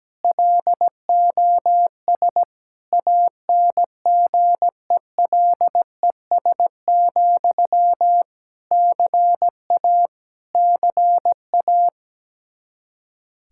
Morse code, heard as a keyed tone, about 17 wpm, 700 Hz. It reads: LOS ANGELES, CA CA
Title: Practice QSO - Exchange 1 - Beginning - 17wpm